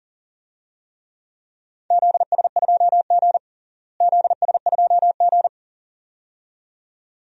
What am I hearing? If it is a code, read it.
Morse code, heard as a keyed tone, about 40 wpm, 700 Hz. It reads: ZS2G ZS2G
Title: DXCC-100 Call Signs - 4 Characters - Repeated 2x - 40wpm